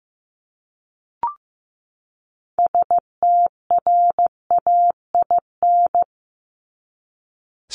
Morse code, heard as a keyed tone, about 15 wpm, 700 Hz. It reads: STRAIN
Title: Top 1751-2000 Words 15wpm